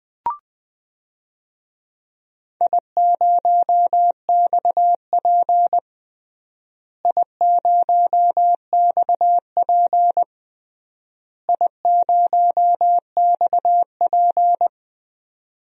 Morse code, heard as a keyed tone, about 21 wpm, 700 Hz. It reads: I0XP I0XP I0XP
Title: DXCC-100 Call Signs - 4 Characters - Repeated 3x - 20wpm